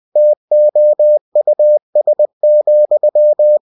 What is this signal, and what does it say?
Morse code, heard as a keyed tone, about 20 wpm, 600 Hz. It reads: TOUS,